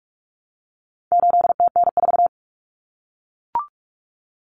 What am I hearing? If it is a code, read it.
Morse code, heard as a keyed tone, about 44 wpm, 700 Hz. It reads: 8TN4
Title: Common Cut Numbers - 4-Digit 45wpm